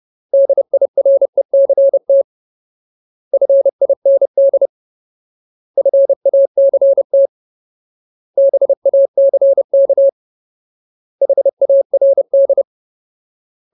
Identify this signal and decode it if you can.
Morse code, heard as a keyed tone, about 30 wpm, 550 Hz. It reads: DIRECT FIND FACT BACK HARD
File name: Ultimate Word List / 30WPM Ultimate